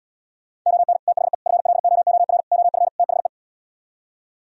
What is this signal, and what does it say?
Morse code, heard as a keyed tone, about 77 wpm, 700 Hz. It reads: ON THE FLOOR OF THE